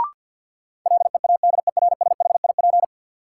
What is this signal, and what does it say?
Morse code, heard as a keyed tone, about 48 wpm, 700 Hz. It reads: LEADERSHIP
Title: Sentences from Top 1600 Words - New Word Drill - 50wpm